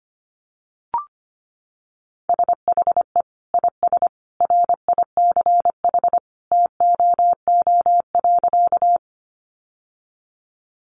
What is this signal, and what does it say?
Morse code, heard as a keyed tone, about 25 wpm, 700 Hz. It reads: SHE IS RICH TOO.
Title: Sentences from Top 1100 Words 25wpm